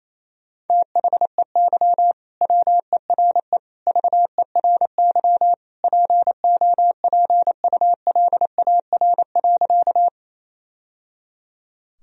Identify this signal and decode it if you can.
Morse code, heard as a keyed tone, about 28 wpm, 700 Hz. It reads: THEY WERE VERY POPULAR.
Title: Sentences from Top 1000 Words - Review Part 3 - 28wpm